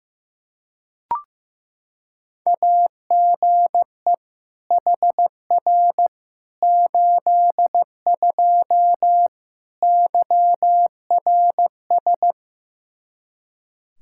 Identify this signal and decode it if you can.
Morse code, heard as a keyed tone, about 15 wpm, 700 Hz. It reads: AGE HR 82 YRS
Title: K1USN SST QSO Practice 15wpm